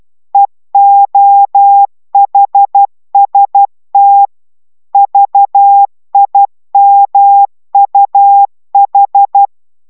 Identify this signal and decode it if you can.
Morse code, heard as a keyed tone, about 12 wpm, 800 Hz. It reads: EOHST VIMUH
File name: taviro tanulas / cw 69 05 10